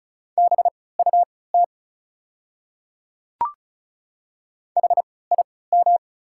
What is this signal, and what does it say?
Morse code, heard as a keyed tone, about 34 wpm, spaced as for 20 wpm, 690 Hz. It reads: BUT HIM
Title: 3x Character Spacing - Top 100 Words - 35wpm